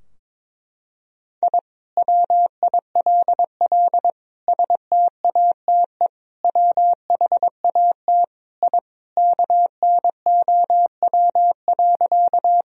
Morse code, heard as a keyed tone, about 21 wpm, 700 Hz. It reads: I WILL STATE WHAT I KNOW.
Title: Sentences from Top 1000 Words - Review Part 2 - 22wpm